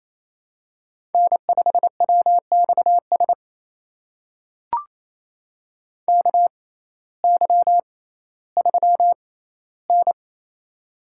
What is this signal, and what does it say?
Morse code, heard as a keyed tone, about 27 wpm, spaced as for 15 wpm, 700 Hz. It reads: N5WXS KY3N
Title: USA Call Signs - 6x Character Spacing - 28wpm